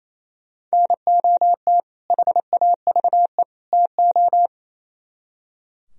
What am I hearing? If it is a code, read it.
Morse code, heard as a keyed tone, about 35 wpm, 700 Hz. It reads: NOT HAVE TO